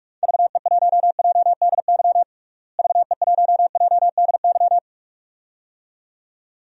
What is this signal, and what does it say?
Morse code, heard as a keyed tone, about 44 wpm, 700 Hz. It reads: VE1JDY VE1JDY
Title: DXCC-100 Call Signs - 6 Characters - Repeated 2x - 45wpm